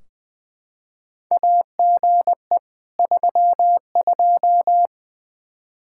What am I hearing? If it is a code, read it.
Morse code, heard as a keyed tone, about 20 wpm, 700 Hz. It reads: AGE 32
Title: Practice QSO - Exchange 3 - More Info - 20wpm